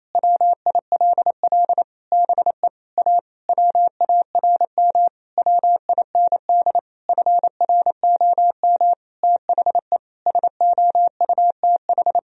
Morse code, heard as a keyed tone, about 28 wpm, 700 Hz. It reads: WILL BE A WARM WIND FROM THE SOUTH
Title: Sentences from Top 1000 Words - Review Part 3 - 28wpm